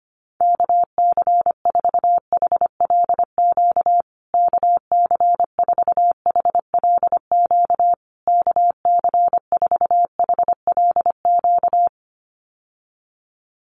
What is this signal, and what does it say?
Morse code, heard as a keyed tone, about 25 wpm, 700 Hz. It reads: KC4HLQ KC4HLQ KC4HLQ
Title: USA Call Signs Repeated 3x 25wpm